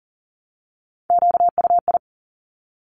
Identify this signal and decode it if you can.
Morse code, heard as a keyed tone, about 38 wpm, 700 Hz. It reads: QUI
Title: Top Three Letter Combinations in Words 40wpm